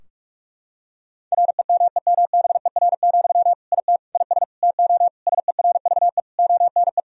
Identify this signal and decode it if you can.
Morse code, heard as a keyed tone, about 41 wpm, 700 Hz. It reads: REMEMBER, IT IS TO SERVE ONE
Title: Sentences from Top 1000 Words - Review Part 2 - 45wpm